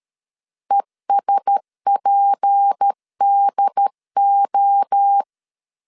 Morse code, heard as a keyed tone, about 13 wpm, 800 Hz. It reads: ESPDO
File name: LEZIONI / 7 GF9Z / 7 GF9 12wpm 2319